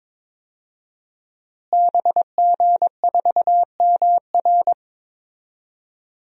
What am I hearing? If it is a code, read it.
Morse code, heard as a keyed tone, about 22 wpm, 700 Hz. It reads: BG4MR
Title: DXCC-100 Call Signs - 5 Characters - Repeated 2x - 22wpm